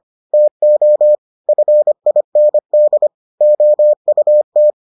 Morse code, heard as a keyed tone, about 25 wpm, 600 Hz. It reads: TO FIND OUT